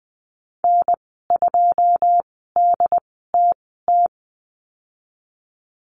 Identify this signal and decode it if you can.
Morse code, heard as a keyed tone, about 20 wpm, spaced as for 15 wpm, 700 Hz. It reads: N2DTT
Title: 122 Lesson 20g - Number 3 - Call Signs 20wpm